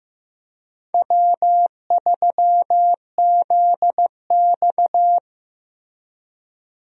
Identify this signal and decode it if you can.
Morse code, heard as a keyed tone, about 15 wpm, 700 Hz. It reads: W3ZX
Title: USA Call Signs - 4 Characters - Repeated 3x - 15wpm